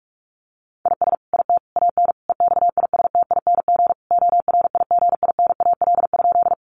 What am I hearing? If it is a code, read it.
Morse code, heard as a keyed tone, about 44 wpm, 700 Hz. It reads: IS IT AN EXISTING ORIGINAL?
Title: Sentences from Top 1500 Words 45wpm